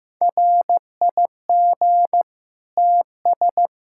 Morse code, heard as a keyed tone, about 15 wpm, 700 Hz. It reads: RIG TS
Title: Practice QSO - Exchange 2 - Info - 15wpm